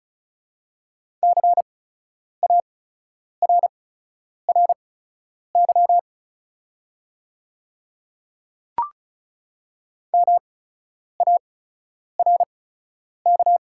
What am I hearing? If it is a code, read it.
Morse code, heard as a keyed tone, about 36 wpm, spaced as for 10 wpm, 690 Hz. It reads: CARRY MARK
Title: 8x Character Spacing - Top 300 Words - 35wpm